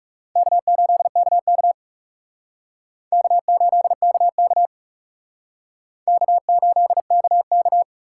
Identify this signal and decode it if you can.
Morse code, heard as a keyed tone, about 37 wpm, 700 Hz. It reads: K8KK K8KK K8KK
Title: Speed Racing - USA 1x2 Call Signs 30wpm